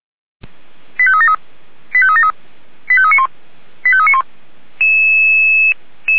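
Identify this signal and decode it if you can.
Five-tone selective call (ZVEI), heard as quick runs of five tones, tones of about 70 ms. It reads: 85373 85373 85391 85391
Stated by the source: ZVEI five-tone call received by SDR.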